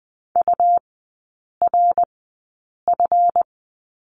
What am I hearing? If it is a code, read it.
Morse code, heard as a keyed tone, about 20 wpm, spaced as for 8 wpm, 700 Hz. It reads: URF
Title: 138 Lesson 23e - Letter f - Three Characters 20wpm